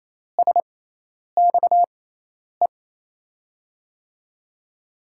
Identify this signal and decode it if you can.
Morse code, heard as a keyed tone, about 28 wpm, spaced as for 10 wpm, 700 Hz. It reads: SXE
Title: USA Call Signs - 6 Characters - 6x Character Spacing - 28wpm